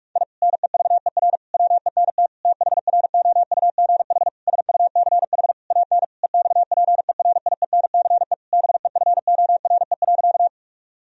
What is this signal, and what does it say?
Morse code, heard as a keyed tone, about 46 wpm, 700 Hz. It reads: I NEVER WENT THROUGH SUCH AN EXPERIENCE BEFORE.
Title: Sentences from Top 1000 Words - Review Part 3 - 45wpm